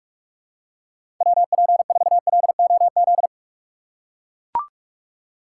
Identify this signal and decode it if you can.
Morse code, heard as a keyed tone, about 44 wpm, 700 Hz. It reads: WP4LOZ
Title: USA Call Signs - 6x Character Spacing - 45wpm